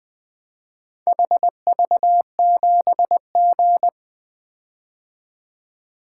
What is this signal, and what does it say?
Morse code, heard as a keyed tone, about 24 wpm, 700 Hz. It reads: HV7G
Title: Speed-Racing - Four Letter-Number Combinations - 20wpm